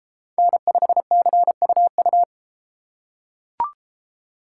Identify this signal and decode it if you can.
Morse code, heard as a keyed tone, about 32 wpm, 700 Hz. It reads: N5CUU